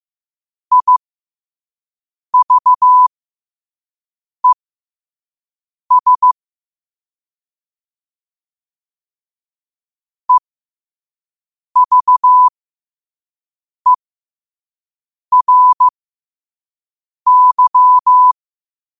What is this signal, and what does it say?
Morse code, heard as a keyed tone, about 15 wpm, spaced as for 5 wpm, 1000 Hz. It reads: IVES EVERY
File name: qotdfiles / 5wpm